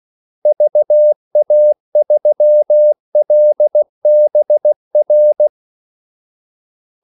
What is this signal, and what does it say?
Morse code, heard as a keyed tone, about 16 wpm, 600 Hz. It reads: VA3LBR
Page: Kaldesignaler 16 wpm